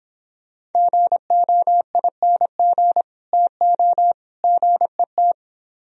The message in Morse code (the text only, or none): GOING TO GET